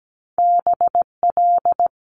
Morse code, heard as a keyed tone, about 17 wpm, 700 Hz. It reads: BL
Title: Multi-letter Phonemes Quick Fire 17wpm